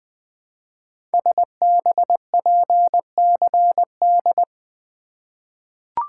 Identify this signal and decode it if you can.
Morse code, heard as a keyed tone, about 20 wpm, 700 Hz. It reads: SBPCD
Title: Code Groups - 5 Letters - 20wpm